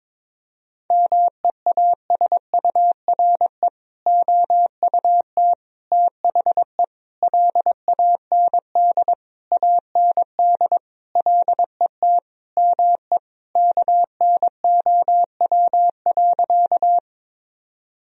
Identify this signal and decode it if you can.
Morse code, heard as a keyed tone, about 22 wpm, 700 Hz. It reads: MEASURE OUT THE LAND AND LET ME KNOW.
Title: Sentences from Top 1000 Words - Review Part 2 - 22wpm